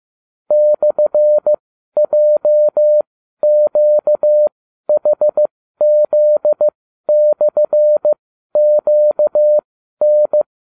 Morse code, heard as a keyed tone, about 15 wpm, spaced as for 12 wpm, 600 Hz. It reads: /JQHZ/QN